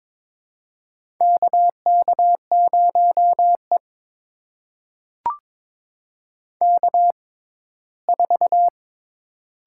Morse code, heard as a keyed tone, about 22 wpm, spaced as for 15 wpm, 700 Hz. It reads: KK0E K4
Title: USA Call Signs - 4 Characters - 6x Character Spacing - 22wpm